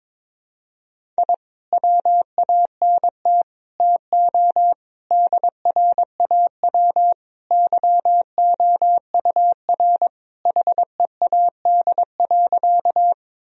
Morse code, heard as a keyed tone, about 22 wpm, 700 Hz. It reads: I WANT TO DRAW YOUR HEAD.
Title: Sentences from Top 1000 Words - Review Part 1 - 22wpm